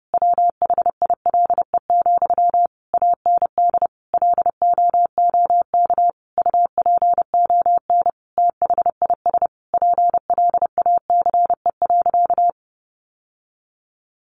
Morse code, heard as a keyed tone, about 30 wpm, 700 Hz. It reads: WHILE, AND LOOK UPON THIS PLACE.
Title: Sentences from Top 1100 Words 30wpm